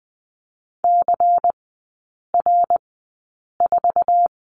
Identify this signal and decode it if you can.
Morse code, heard as a keyed tone, about 20 wpm, spaced as for 8 wpm, 700 Hz. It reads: CR4
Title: 093 Lesson 16e - Letter c - Three Characters 20wpm